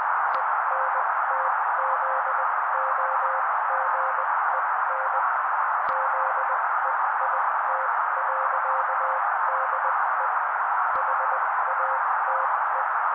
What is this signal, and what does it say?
Morse code, heard as a keyed tone, about 20 wpm, spaced as for 16 wpm, 530 Hz. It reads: ENTZOGEN ZEIT.DE HATE
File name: daily / 16wpm noise